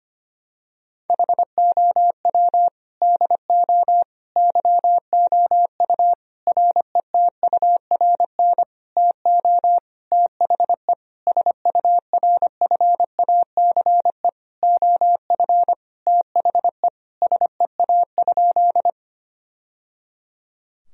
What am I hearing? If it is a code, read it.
Morse code, heard as a keyed tone, about 25 wpm, 700 Hz. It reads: HOW DO YOU RETURN TO THE SURFACE OF THE SEA?
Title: Sentences from Top 1000 Words - Review Part 3 - 25wpm